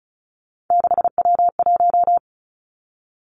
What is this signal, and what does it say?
Morse code, heard as a keyed tone, about 36 wpm, 700 Hz. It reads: 6W1
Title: Three Challenging Letter-Number Combinations 35wpm